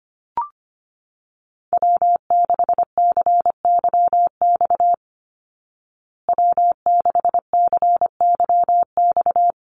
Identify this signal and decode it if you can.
Morse code, heard as a keyed tone, about 25 wpm, 700 Hz. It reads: W6CYX W6CYX
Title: Call Signs Repeated 3x - 25wpm